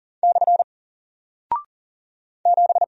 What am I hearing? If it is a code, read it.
Morse code, heard as a keyed tone, about 38 wpm, 700 Hz. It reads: / 7
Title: Single Character (Letter:Number:Punctuation:Prosign) Alt 40wpm